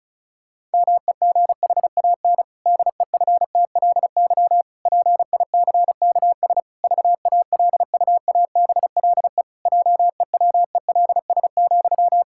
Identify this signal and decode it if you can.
Morse code, heard as a keyed tone, about 36 wpm, 700 Hz. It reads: MEGHAN DEFTLY PICKS VALUABLE JEWELS,
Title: Pangram Challenge v2 - 35wpm